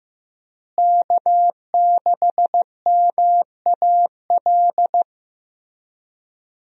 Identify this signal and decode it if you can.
Morse code, heard as a keyed tone, about 15 wpm, 700 Hz. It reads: K6MAL
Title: USA Call Signs - 5 Characters - Repeated 3x - 15wpm